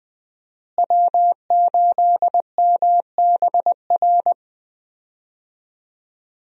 Morse code, heard as a keyed tone, about 20 wpm, 700 Hz. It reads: W8MBR
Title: USA Call Signs - 5 Characters - Repeated 2x - 20wpm